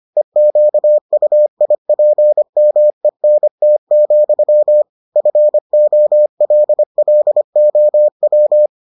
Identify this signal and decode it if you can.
Morse code, heard as a keyed tone, about 25 wpm, 600 Hz. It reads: EQUIPMENT, FOLLOW